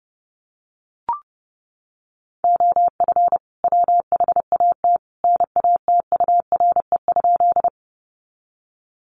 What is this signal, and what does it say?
Morse code, heard as a keyed tone, about 30 wpm, 700 Hz. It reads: OF WHAT NATURE?
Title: Sentences from Top 600 Words 30wpm